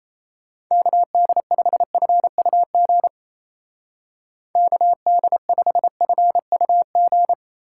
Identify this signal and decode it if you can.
Morse code, heard as a keyed tone, about 37 wpm, 700 Hz. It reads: KD5FUG KD5FUG